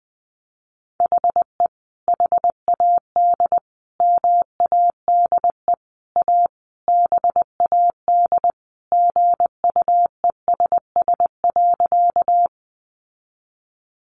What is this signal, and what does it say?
Morse code, heard as a keyed tone, about 20 wpm, 700 Hz. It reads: HE HAD MADE A BAD GUESS.
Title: Sentences from Top 1500 Words 20wpm